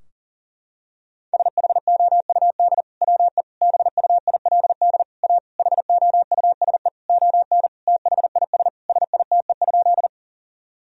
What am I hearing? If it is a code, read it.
Morse code, heard as a keyed tone, about 40 wpm, 700 Hz. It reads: SHOUD WE BUILD A HOUSE ON THIS SITE?
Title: Sentences from Top 1000 Words - Review Part 4 - 40wpm